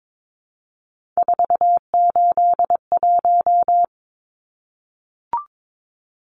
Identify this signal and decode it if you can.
Morse code, heard as a keyed tone, about 22 wpm, 700 Hz. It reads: 481
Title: Numbers - 3-Digit 22wpm